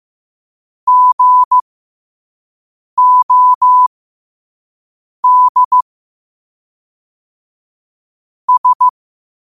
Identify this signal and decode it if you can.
Morse code, heard as a keyed tone, about 15 wpm, spaced as for 5 wpm, 1000 Hz. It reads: GOD S
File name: qotdfiles / 5wpm